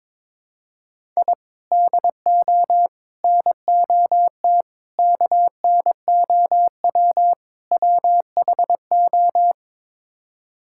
Morse code, heard as a keyed tone, about 22 wpm, 700 Hz. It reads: I DO NOT KNOW WHO
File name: practice / 22wpm / Build A Phrase - 5-Word Phrases - 22wpm